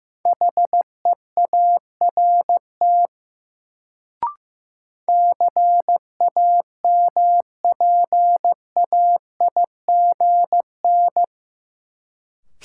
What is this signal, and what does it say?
Morse code, heard as a keyed tone, about 15 wpm, 700 Hz. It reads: HEART CAMPAIGN
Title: Sentences from Top 1200 Words - New Word Drill Alt - 15wpm